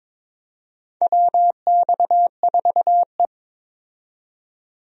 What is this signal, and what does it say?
Morse code, heard as a keyed tone, about 22 wpm, 700 Hz. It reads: WX4E
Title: USA Call Signs - 4 Characters - 22wpm